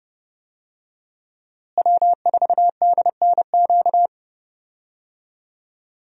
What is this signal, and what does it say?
Morse code, heard as a keyed tone, about 31 wpm, 700 Hz. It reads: W4DNQ
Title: USA Call Signs - 5 Characters - Repeated 3x - 30wpm